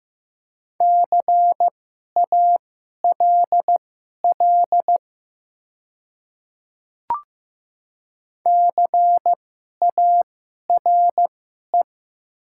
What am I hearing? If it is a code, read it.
Morse code, heard as a keyed tone, about 15 wpm, spaced as for 11 wpm, 700 Hz. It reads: CALL CARE
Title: 2x Character Spacing - Top 300 Words - 15wpm